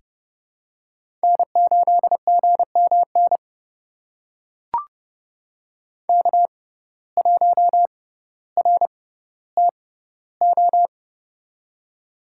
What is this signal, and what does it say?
Morse code, heard as a keyed tone, about 30 wpm, spaced as for 15 wpm, 700 Hz. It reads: N8GMN K1RTO
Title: USA Call Signs - 6x Character Spacing - 30wpm